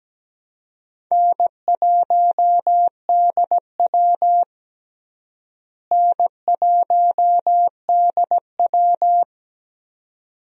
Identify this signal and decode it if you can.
Morse code, heard as a keyed tone, about 17 wpm, 700 Hz. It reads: N1DW N1DW
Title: USA Call Signs - 4 Characters - Repeated 2x - 17wpm